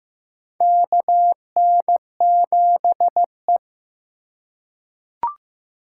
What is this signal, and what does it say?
Morse code, heard as a keyed tone, about 15 wpm, 700 Hz. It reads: KN7E
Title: USA Call Signs - 4 Characters - 2x Character Spacing - 15wpm